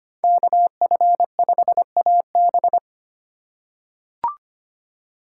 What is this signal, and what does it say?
Morse code, heard as a keyed tone, about 25 wpm, 700 Hz. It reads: KF5AB
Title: USA Call Signs - 5 Characters - 6x Character Spacing - 25wpm